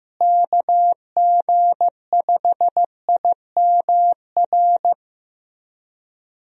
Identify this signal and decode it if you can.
Morse code, heard as a keyed tone, about 15 wpm, 700 Hz. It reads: KG5IMR
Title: USA Call Signs - 6 Characters - Repeated 2x - 15wpm